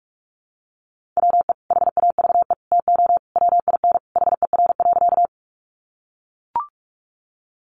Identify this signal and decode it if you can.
Morse code, heard as a keyed tone, about 44 wpm, 700 Hz. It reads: WE HAVE TO WIN HER.
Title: Sentences from Top 1500 Words 45wpm